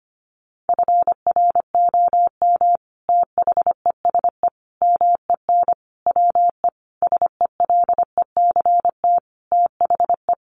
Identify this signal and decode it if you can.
Morse code, heard as a keyed tone, about 25 wpm, 700 Hz. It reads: FROM THESE MEN WE SELECT THE
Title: Sentences from Top 1300 Words 25wpm